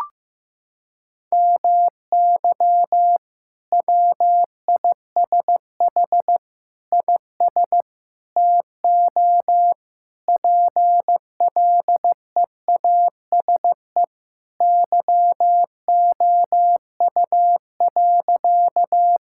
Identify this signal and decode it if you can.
Morse code, heard as a keyed tone, about 16 wpm, 700 Hz. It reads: MY WISH IS TO PLEASE YOU.
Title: Sentences from Top 1600 Words 15wpm